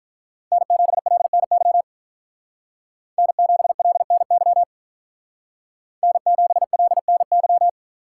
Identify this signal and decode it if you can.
Morse code, heard as a keyed tone, about 43 wpm, 700 Hz. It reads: N7LNY N7LNY N7LNY
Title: Speed Racing - USA 1x3 Call Signs 35wpm